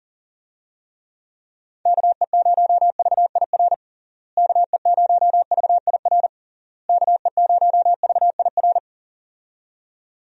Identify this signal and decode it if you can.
Morse code, heard as a keyed tone, about 40 wpm, 700 Hz. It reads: KE0VIR KE0VIR KE0VIR
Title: USA Call Signs - 6 Characters - Repeated 3x - 40wpm